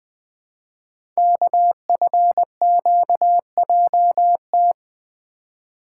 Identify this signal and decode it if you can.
Morse code, heard as a keyed tone, about 20 wpm, 700 Hz. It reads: KFQJT
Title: Code Groups - 5 Letters - 20wpm